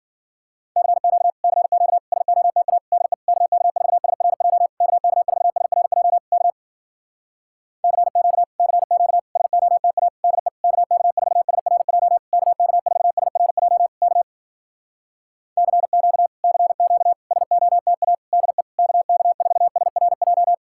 Morse code, heard as a keyed tone, about 60 wpm, 700 Hz. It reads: CQ CQ SOTA DE KK4SRJ KK4SRJ K CQ CQ SOTA DE KK4SRJ KK4SRJ K CQ CQ SOTA DE KK4SRJ